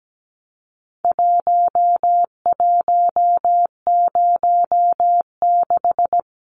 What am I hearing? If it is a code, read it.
Morse code, heard as a keyed tone, about 17 wpm, 700 Hz. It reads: 1106
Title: Numbers - Contest Style - 4-Digit 17wpm